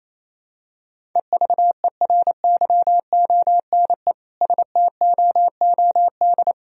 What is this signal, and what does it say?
Morse code, heard as a keyed tone, about 28 wpm, 700 Hz. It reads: EVERYONE STOOD
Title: Sentences from Top 1000 Words - Review Part 2 - 28wpm